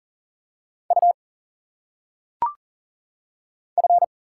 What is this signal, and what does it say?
Morse code, heard as a keyed tone, about 42 wpm, 700 Hz. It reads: U F
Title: Single Letters - Phonetic Pronunciation 40wpm